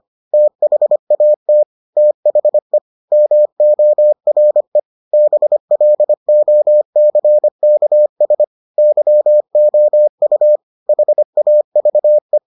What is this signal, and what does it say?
Morse code, heard as a keyed tone, about 25 wpm, 600 Hz. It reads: THAT THE MORE BLOCKS YOU HAVE